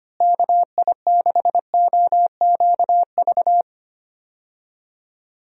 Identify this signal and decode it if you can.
Morse code, heard as a keyed tone, about 25 wpm, 700 Hz. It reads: KI6OQV
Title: USA Call Signs - 6 Characters - Repeated 2x - 25wpm